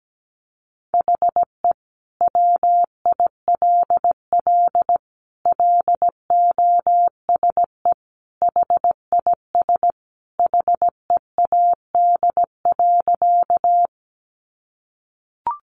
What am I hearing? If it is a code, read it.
Morse code, heard as a keyed tone, about 17 wpm, 700 Hz. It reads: HE WILL LOSE HIS HEAD.
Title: Sentences from Top 1100 Words 17wpm